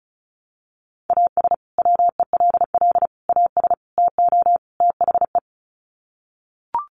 Morse code, heard as a keyed tone, about 34 wpm, 700 Hz. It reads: AS WELL AS TO THE
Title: Sets of 5 Words Encore2 35wpm